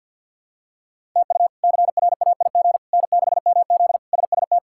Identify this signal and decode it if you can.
Morse code, heard as a keyed tone, about 52 wpm, 700 Hz. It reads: TU CRAIG N6MZ SST
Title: K1USN SST QSO Practice - 50wpm